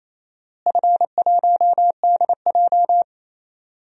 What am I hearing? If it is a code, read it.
Morse code, heard as a keyed tone, about 27 wpm, 700 Hz. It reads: F1DJ
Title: DXCC-100 Call Signs - 4 Characters - Repeated 3x - 28wpm